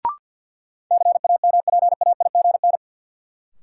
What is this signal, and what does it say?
Morse code, heard as a keyed tone, about 53 wpm, 700 Hz. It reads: CAMPAIGN
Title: Sentences from Top 1200 Words - New Word Drill - 50wpm